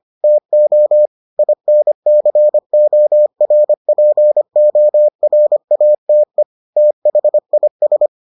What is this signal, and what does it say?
Morse code, heard as a keyed tone, about 25 wpm, 600 Hz. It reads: TO INCORPORATE THIS